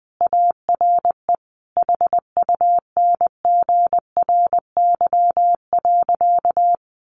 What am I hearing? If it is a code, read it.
Morse code, heard as a keyed tone, about 20 wpm, 700 Hz. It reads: ARE HUNGRY.
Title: A Sound Of Thunder 20wpm